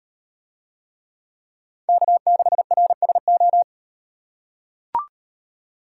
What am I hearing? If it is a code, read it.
Morse code, heard as a keyed tone, about 38 wpm, 700 Hz. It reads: K6RSO